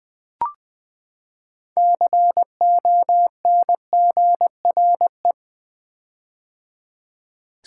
Morse code, heard as a keyed tone, about 20 wpm, 700 Hz. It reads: CONGRE
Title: Build A Word - 6 Letters At A Time - 12-Letter Words - 20wpm